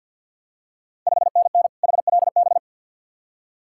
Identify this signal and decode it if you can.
Morse code, heard as a keyed tone, about 48 wpm, 700 Hz. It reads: 5NN HLB
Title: Ontario QSO Party Exchange 50wpm